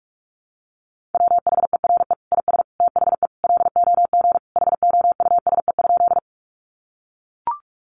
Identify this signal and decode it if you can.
Morse code, heard as a keyed tone, about 44 wpm, 700 Hz. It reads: WHERE IS THE LOG HOUSE?
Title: Sentences from Top 1400 Words 45wpm